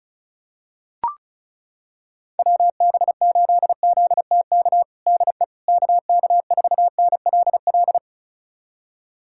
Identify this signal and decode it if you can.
Morse code, heard as a keyed tone, about 34 wpm, 700 Hz. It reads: WB8ZTK DE KK4NLL
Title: USA Callsign DE Callsign - 6 Characters - 35wpm